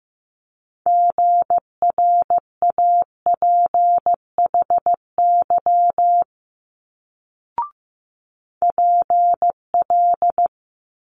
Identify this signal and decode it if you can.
Morse code, heard as a keyed tone, about 15 wpm, 700 Hz. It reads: GRAPHY PL
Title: Multi-letter Phonemes Quick Fire 15wpm